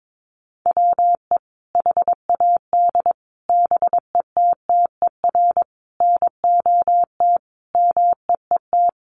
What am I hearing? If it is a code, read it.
Morse code, heard as a keyed tone, about 22 wpm, 700 Hz. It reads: WE HAD BETTER NOT MEET
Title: Sentences from Top 1100 Words 22wpm